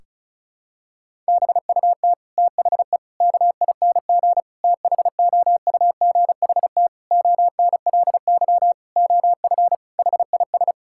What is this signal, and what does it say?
Morse code, heard as a keyed tone, about 36 wpm, 700 Hz. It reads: BUT THE KING THOUGHT ONLY OF HIS
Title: Sentences from Top 1600 Words 35wpm